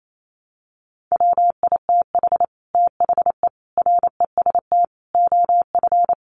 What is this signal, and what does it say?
Morse code, heard as a keyed tone, about 28 wpm, 700 Hz. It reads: WITH THE REST OF
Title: Sets of 5 Words 28wpm